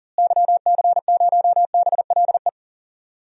Morse code, heard as a keyed tone, about 40 wpm, 700 Hz. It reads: YC0BLE
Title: DXCC-100 Call Signs - 6 Characters - Repeated 3x - 40wpm